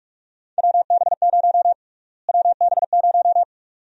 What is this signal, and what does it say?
Morse code, heard as a keyed tone, about 46 wpm, 700 Hz. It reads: WB0 WB0
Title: USA Call Sign Prefixes - Repeated 2x - 45wpm